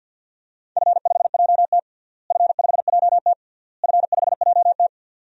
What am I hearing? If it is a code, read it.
Morse code, heard as a keyed tone, about 46 wpm, 700 Hz. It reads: F5JT F5JT F5JT
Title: DXCC-100 Call Signs - 4 Characters - Repeated 3x - 50wpm